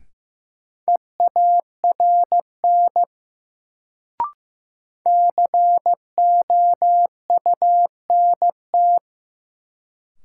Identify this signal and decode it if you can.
Morse code, heard as a keyed tone, about 15 wpm, 700 Hz. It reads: EARN COUNT
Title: Sentences from Top 1500 Words - New Word Drill - 15wpm